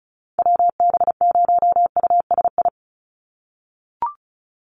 Morse code, heard as a keyed tone, about 34 wpm, 700 Hz. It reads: WB0USI
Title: NA SOTA Activator Callsigns 35wpm